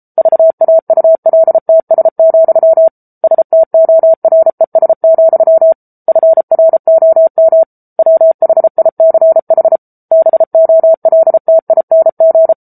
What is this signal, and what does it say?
Morse code, heard as a keyed tone, about 34 wpm, 650 Hz. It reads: VAULTS, STORES, FROM WHICH BOLTING